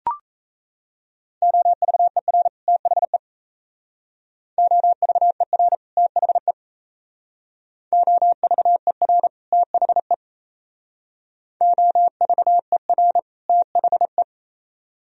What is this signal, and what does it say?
Morse code, heard as a keyed tone, about 37 wpm, 700 Hz. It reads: OVER THE OVER THE OVER THE OVER THE